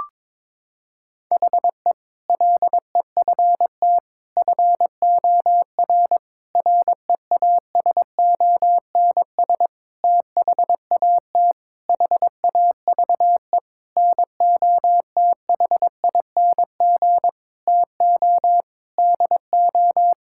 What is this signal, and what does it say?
Morse code, heard as a keyed tone, about 22 wpm, 700 Hz. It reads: HE LEFT FOR REASONS THAT HAVE NOTHING TO DO
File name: practice / 22wpm / Sentences with 8-Word Phrases - 22wpm